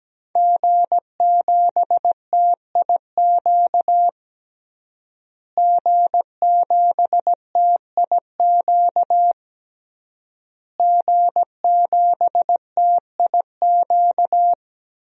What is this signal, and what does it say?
Morse code, heard as a keyed tone, about 17 wpm, 700 Hz. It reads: G7TIQ G7TIQ G7TIQ
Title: DXCC-100 Call Signs - 5 Characters - Repeated 3x - 17wpm